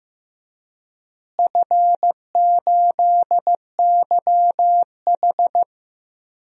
Morse code, heard as a keyed tone, about 15 wpm, 700 Hz. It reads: F8YH
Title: DXCC-100 Call Signs - 4 Characters - Repeated 3x - 15wpm